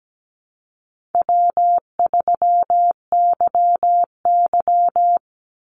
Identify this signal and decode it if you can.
Morse code, heard as a keyed tone, about 17 wpm, 700 Hz. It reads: W3YY
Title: Easy Call Signs - 17wpm